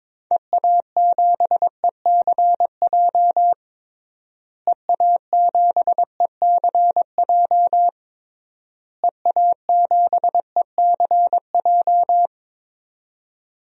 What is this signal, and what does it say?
Morse code, heard as a keyed tone, about 21 wpm, 700 Hz. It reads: EA7ECJ EA7ECJ EA7ECJ
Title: DXCC-100 Call Signs - 6 Characters - Repeated 3x - 22wpm